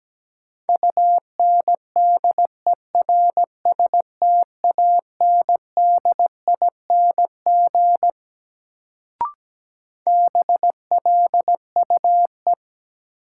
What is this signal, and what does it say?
Morse code, heard as a keyed tone, about 17 wpm, 700 Hz. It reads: UNDERSTANDING BLUE
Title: Sentences from Top 1000 Words - Review Part 3 - New Word Drill - 17wpm